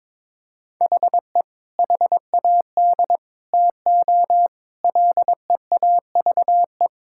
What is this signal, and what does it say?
Morse code, heard as a keyed tone, about 22 wpm, 700 Hz. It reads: HE HAD TO LEAVE
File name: practice / 22wpm / Sentences with 2-Word Phrases - 22wpm